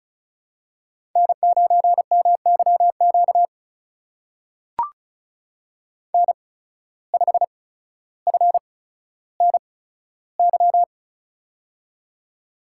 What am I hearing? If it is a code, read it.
Morse code, heard as a keyed tone, about 36 wpm, spaced as for 15 wpm, 700 Hz. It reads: N9MYQ N5FNY
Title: USA Call Signs - 5 Characters - 8x Character Spacing - 35wpm